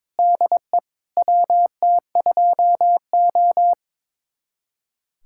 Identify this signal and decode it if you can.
Morse code, heard as a keyed tone, about 22 wpm, 700 Hz. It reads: DE WT2O
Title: USA Callsign DE Callsign - 4 Characters - 22wpm